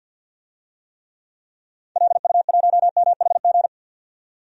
Morse code, heard as a keyed tone, about 51 wpm, 700 Hz. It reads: LU1MHG
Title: DXCC-100 Call Signs - 6 Characters - 50wpm